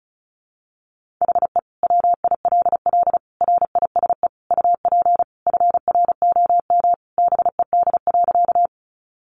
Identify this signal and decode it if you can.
Morse code, heard as a keyed tone, about 34 wpm, 700 Hz. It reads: HE WILL RISE UP FROM BED.
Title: Sentences from Top 900 Words 35wpm